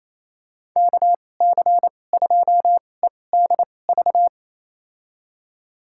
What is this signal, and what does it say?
Morse code, heard as a keyed tone, about 28 wpm, spaced as for 20 wpm, 700 Hz. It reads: KC2EDV
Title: USA Call Signs - 2x Character Spacing - 28wpm